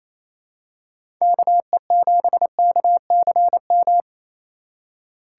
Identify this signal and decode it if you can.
Morse code, heard as a keyed tone, about 28 wpm, 700 Hz. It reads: KE7KCM
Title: USA Call Signs - 3x Character Spacing - 28wpm